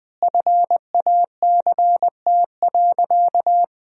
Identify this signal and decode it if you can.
Morse code, heard as a keyed tone, about 20 wpm, 700 Hz. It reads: FACT.
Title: 2x Word-Spacing - Sentences from Top 500 Words - 20wpm